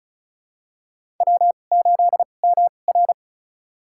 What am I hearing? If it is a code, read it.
Morse code, heard as a keyed tone, about 36 wpm, spaced as for 25 wpm, 700 Hz. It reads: W8MR
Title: USA Call Signs - 4 Characters - 2x Character Spacing - 35wpm